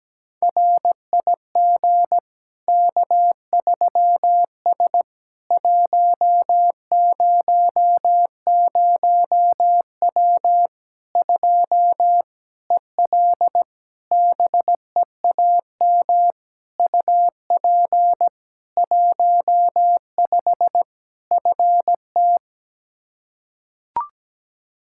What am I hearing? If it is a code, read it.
Morse code, heard as a keyed tone, about 17 wpm, 700 Hz. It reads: RIG K3S 100W 2 EL BEAM UP 15 FT
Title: Practice QSO - Exchange 2 - Info - 17wpm